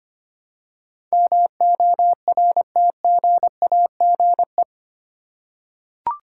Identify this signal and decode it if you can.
Morse code, heard as a keyed tone, about 25 wpm, 700 Hz. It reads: MORTGAGE
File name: Eight Letter Words - 25wpm